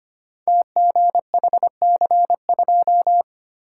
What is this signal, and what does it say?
Morse code, heard as a keyed tone, about 25 wpm, 700 Hz. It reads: TGHC2
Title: Code Groups - 5 Letter:Numbers - 25wpm